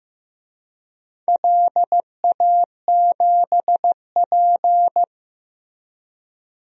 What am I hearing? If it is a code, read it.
Morse code, heard as a keyed tone, about 15 wpm, 700 Hz. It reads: LA7P
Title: DXCC-100 Call Signs - 4 Characters - Repeated 2x - 15wpm